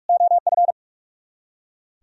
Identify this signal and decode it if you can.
Morse code, heard as a keyed tone, about 58 wpm, 700 Hz. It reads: OF